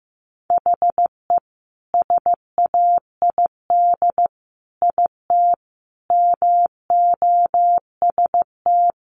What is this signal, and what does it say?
Morse code, heard as a keyed tone, about 15 wpm, 700 Hz. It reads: HE SAID IT MOST
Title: Sentences from Top 1200 Words 15wpm